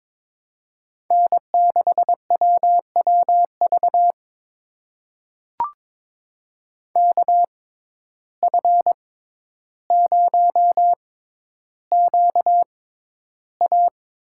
Morse code, heard as a keyed tone, about 22 wpm, spaced as for 11 wpm, 700 Hz. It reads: N6WWV KF0QA
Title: USA Call Signs - 5 Characters - 6x Character Spacing - 22wpm